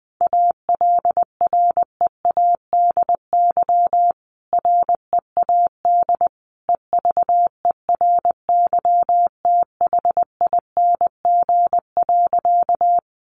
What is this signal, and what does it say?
Morse code, heard as a keyed tone, about 20 wpm, 700 Hz. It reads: ALREADY READ EVERYTHING.
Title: Sentences from Top 700 Words 20wpm